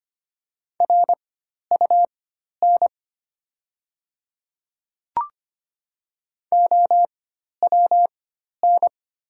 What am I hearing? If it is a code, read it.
Morse code, heard as a keyed tone, about 25 wpm, spaced as for 12 wpm, 700 Hz. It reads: RUN OWN
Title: 4x Character Spacing - Top 300 Words - 25wpm